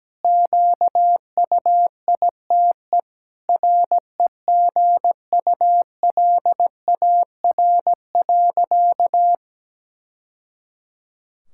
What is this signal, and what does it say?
Morse code, heard as a keyed tone, about 17 wpm, 700 Hz. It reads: QUITE REGULAR.
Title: Sentences from Top 1000 Words - Review Part 6 - 17wpm